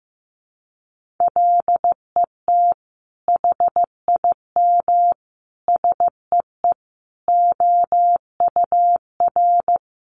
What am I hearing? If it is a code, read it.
Morse code, heard as a keyed tone, about 15 wpm, 700 Hz. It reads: LET HIM SEE OUR
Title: Sentences from Top 1400 Words 15wpm